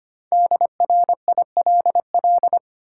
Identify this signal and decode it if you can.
Morse code, heard as a keyed tone, about 25 wpm, 700 Hz. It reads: DRILL
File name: practice / 25wpm / Easy Words with Hints - 25wpm